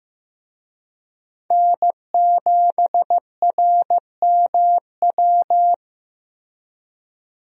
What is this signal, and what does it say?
Morse code, heard as a keyed tone, about 15 wpm, 700 Hz. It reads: N7RMW
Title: USA Call Signs - 5 Characters - Repeated 3x - 15wpm